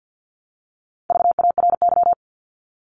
Morse code, heard as a keyed tone, about 50 wpm, 700 Hz. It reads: VARY
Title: Top 1001-1250 Words 50wpm